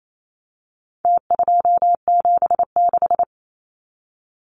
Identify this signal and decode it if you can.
Morse code, heard as a keyed tone, about 28 wpm, 700 Hz. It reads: T276
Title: Numbers - Contest Style - 4-Digit 28wpm